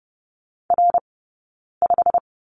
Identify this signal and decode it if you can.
Morse code, heard as a keyed tone, about 30 wpm, spaced as for 9 wpm, 690 Hz. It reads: R5
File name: Lesson 17d - Letter u - Two Characters 30wpm